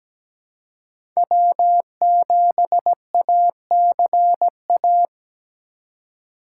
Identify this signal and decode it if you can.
Morse code, heard as a keyed tone, about 17 wpm, 700 Hz. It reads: W7ACA
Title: USA Call Signs - 5 Characters - 3x Character Spacing - 17wpm